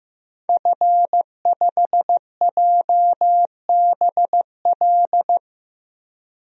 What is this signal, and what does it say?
Morse code, heard as a keyed tone, about 15 wpm, 700 Hz. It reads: F5JBL
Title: DXCC-100 Call Signs - 5 Characters - 15wpm